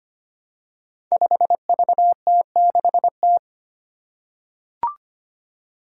Morse code, heard as a keyed tone, about 25 wpm, 700 Hz. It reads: 5VT6T
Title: Five Challenging Letter-Number Combinations 25wpm